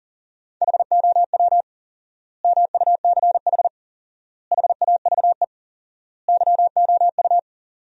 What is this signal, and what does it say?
Morse code, heard as a keyed tone, about 40 wpm, 700 Hz. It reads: HOW MUCH HAVE YOU
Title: 4x Word-Spacing - Sentences from Top 300 Words - 40wpm